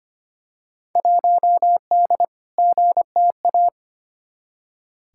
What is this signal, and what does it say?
Morse code, heard as a keyed tone, about 25 wpm, 700 Hz. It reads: 1D GTA
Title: Field Day Exchanges - 25wpm